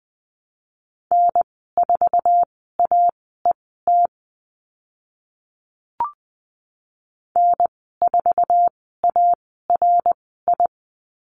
Morse code, heard as a keyed tone, about 20 wpm, spaced as for 14 wpm, 700 Hz. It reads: N4AET N4ARI
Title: 076 Lesson 13g - Letter L - Call Signs 20wpm